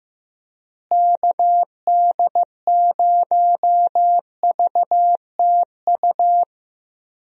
Morse code, heard as a keyed tone, about 15 wpm, 700 Hz. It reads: KD0VTU
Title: USA Call Signs - 6 Characters - Repeated 3x - 15wpm